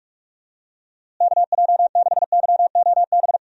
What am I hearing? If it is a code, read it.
Morse code, heard as a keyed tone, about 44 wpm, 700 Hz. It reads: KJ6YOB
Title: USA Call Signs - 6 Characters - Repeated 2x - 45wpm